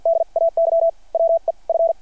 Morse code, heard as a keyed tone, about 53 wpm, spaced as for 46 wpm, 660 Hz. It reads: D AY WE F
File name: archived / CRSnet